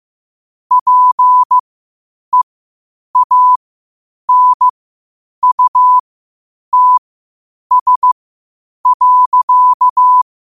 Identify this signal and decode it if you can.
Morse code, heard as a keyed tone, about 15 wpm, spaced as for 8 wpm, 1000 Hz. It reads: PEANUTS.
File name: qotdfiles / 8wpm